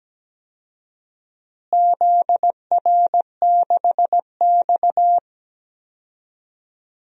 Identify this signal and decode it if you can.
Morse code, heard as a keyed tone, about 17 wpm, 700 Hz. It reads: ZR6X
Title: DXCC-100 Call Signs - 4 Characters - Repeated 3x - 17wpm